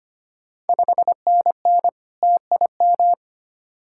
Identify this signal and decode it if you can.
Morse code, heard as a keyed tone, about 25 wpm, 700 Hz. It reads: 5NN TIM
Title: Ontario QSO Party Exchange 25wpm